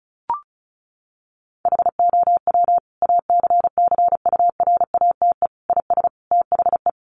Morse code, heard as a keyed tone, about 34 wpm, 700 Hz. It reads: HOW ACCURATE IS THE
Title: Sentences from Top 1200 Words 35wpm